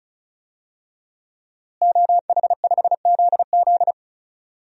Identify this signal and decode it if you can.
Morse code, heard as a keyed tone, about 34 wpm, 700 Hz. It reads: OH5ZZ
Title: DXCC-100 Call Signs - 5 Characters - Repeated 3x - 35wpm